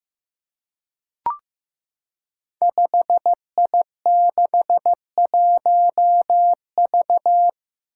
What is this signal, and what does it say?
Morse code, heard as a keyed tone, about 15 wpm, 700 Hz. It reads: 5I61V
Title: Five Challenging Letter-Number Combinations 15wpm